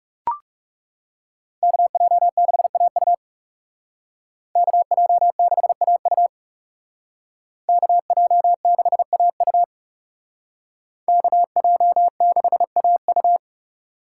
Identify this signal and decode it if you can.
Morse code, heard as a keyed tone, about 37 wpm, 700 Hz. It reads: KJ6AU KJ6AU KJ6AU KJ6AU
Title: Speed Racing - USA Call Signs 30wpm